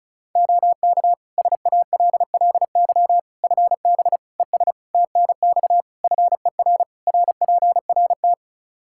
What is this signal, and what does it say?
Morse code, heard as a keyed tone, about 36 wpm, 700 Hz. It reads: OK SALLY FB ES TNX FER RPRT
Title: Practice QSO - Exchange 2 - Info - 35wpm